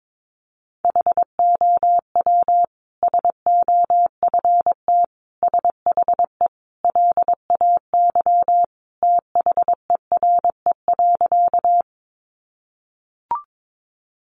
Morse code, heard as a keyed tone, about 22 wpm, 700 Hz. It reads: HOW SOFT SHE LAY THERE.
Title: Sentences from Top 1500 Words 22wpm